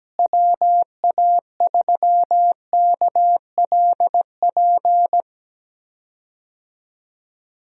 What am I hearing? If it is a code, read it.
Morse code, heard as a keyed tone, about 17 wpm, 700 Hz. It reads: WA3KLP
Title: USA Call Signs - 6 Characters - Repeated 2x - 17wpm